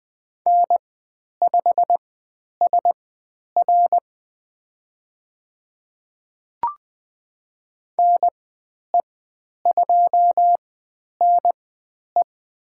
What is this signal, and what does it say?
Morse code, heard as a keyed tone, about 20 wpm, spaced as for 10 wpm, 700 Hz. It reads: N5SR NE2NE
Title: 102 Lesson 17g - Letter u - Call Signs 20wpm